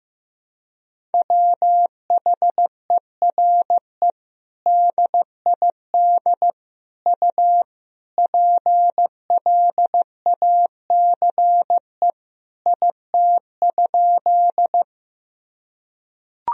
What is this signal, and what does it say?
Morse code, heard as a keyed tone, about 16 wpm, 700 Hz. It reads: WHERE DID U PLACE IT?
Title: Sentences from Top 200 Words and Abbreviations 15wpm